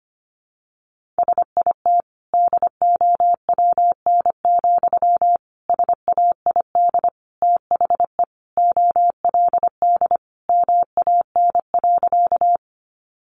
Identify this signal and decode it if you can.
Morse code, heard as a keyed tone, about 25 wpm, 700 Hz. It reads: SIT DOWN, SAID THE OLD MAN.
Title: Sentences from Top 500 Words 25wpm